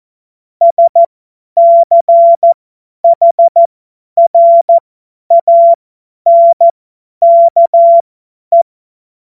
Morse code, heard as a keyed tone, about 14 wpm, spaced as for 10 wpm, 680 Hz. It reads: SCHRANKE
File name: Morsekurs / Witze / Witz 20 Februar 2024 50 F70